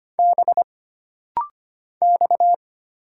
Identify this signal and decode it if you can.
Morse code, heard as a keyed tone, about 25 wpm, 700 Hz. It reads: B X
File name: Single Character (Letter:Number:Punctuation:Prosign) - Rapid-Fire - 25wpm